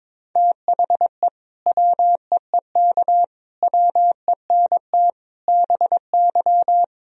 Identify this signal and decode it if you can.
Morse code, heard as a keyed tone, about 22 wpm, 700 Hz. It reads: THE WEEK WENT BY
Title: Sentences from Top 1000 Words - Review Part 2 - 22wpm